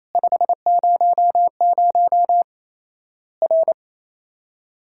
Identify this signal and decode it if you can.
Morse code, heard as a keyed tone, about 28 wpm, 700 Hz. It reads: 500 R
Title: Worked All Europe DX Contest - QTC Exchange - 28wpm